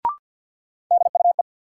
Morse code, heard as a keyed tone, about 53 wpm, 700 Hz. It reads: DUE
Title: Sets of 5 Words Encore - New Word Drill - 50wpm